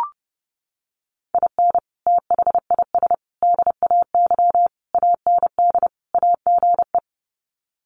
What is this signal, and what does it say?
Morse code, heard as a keyed tone, about 30 wpm, 700 Hz. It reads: IN THIS DAY AND AGE
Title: Sets of 5 Words Encore 30wpm